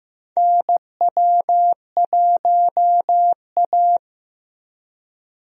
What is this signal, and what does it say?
Morse code, heard as a keyed tone, about 15 wpm, 700 Hz. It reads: NW1A
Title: USA Call Signs - 4 Characters - Repeated 3x - 15wpm